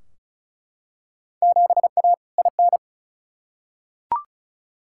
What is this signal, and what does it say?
Morse code, heard as a keyed tone, about 36 wpm, 700 Hz. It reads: 7A IN
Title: Field Day Exchanges Encore - 35wpm